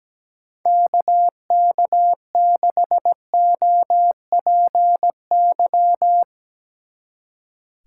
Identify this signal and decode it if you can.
Morse code, heard as a keyed tone, about 17 wpm, 700 Hz. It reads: KK6OPY
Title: Speed Racing - USA 2x3 Call Signs 17wpm